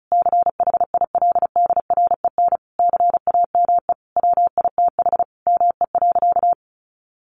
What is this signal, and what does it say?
Morse code, heard as a keyed tone, about 34 wpm, 700 Hz. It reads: CHILDREN CAME WITH ME.
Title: Sentences from Top 1300 Words 35wpm